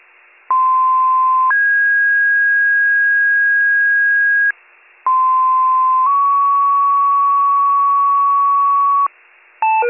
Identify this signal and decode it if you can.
Two-tone page, heard as a one-second tone, then a three-second tone, about 1040 then 1650 Hz, about 1040 then 1120 Hz. It is heard from 0.5 to 9.1 s.